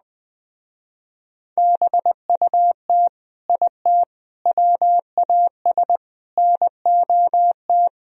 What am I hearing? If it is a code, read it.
Morse code, heard as a keyed tone, about 20 wpm, 700 Hz. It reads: BUT IT WAS NOT
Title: Speed-Racing - Sets of 4 Words 17wpm